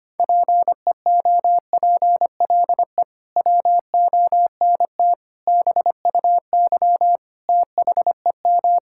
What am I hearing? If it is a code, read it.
Morse code, heard as a keyed tone, about 25 wpm, 700 Hz. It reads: PEOPLE WONT BUY THEM
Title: Sentences from Top 1000 Words - Review Part 6 - 25wpm